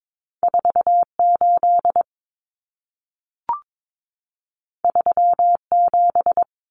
Morse code, heard as a keyed tone, about 22 wpm, 700 Hz. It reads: 48 37
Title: Numbers - 2-Digit 22wpm